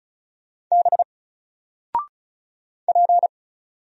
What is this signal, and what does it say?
Morse code, heard as a keyed tone, about 34 wpm, 700 Hz. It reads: B P
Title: Single Letter-Number Alt 35wpm